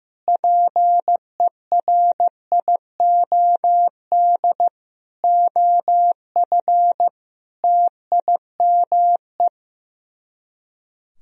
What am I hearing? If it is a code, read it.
Morse code, heard as a keyed tone, about 15 wpm, 700 Hz. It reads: PERIOD OF TIME
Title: Speed-Racing - Sets of 5 Words 15wpm